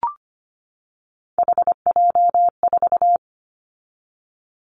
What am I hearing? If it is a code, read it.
Morse code, heard as a keyed tone, about 25 wpm, 700 Hz. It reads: HJ4
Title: Callsign Prefixes for Top 100 DXCC - 25wpm